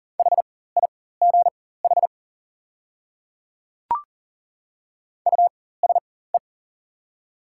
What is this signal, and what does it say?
Morse code, heard as a keyed tone, about 40 wpm, spaced as for 19 wpm, 690 Hz. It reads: HIGH USE
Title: 4x Character Spacing - Top 300 Words - 40wpm